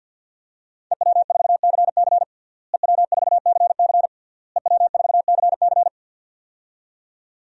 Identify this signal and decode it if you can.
Morse code, heard as a keyed tone, about 49 wpm, 700 Hz. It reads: EW4CC EW4CC EW4CC
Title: DXCC-100 Call Signs - 5 Characters - Repeated 3x - 50wpm